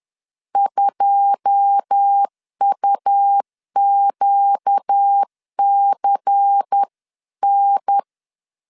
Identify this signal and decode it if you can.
Morse code, heard as a keyed tone, about 10 wpm, 800 Hz. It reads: 2UQCN